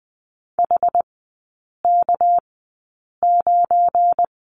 Morse code, heard as a keyed tone, about 20 wpm, spaced as for 8 wpm, 690 Hz. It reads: HK9
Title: 212 Lesson 34e - Number 8 - Three Characters 20wpm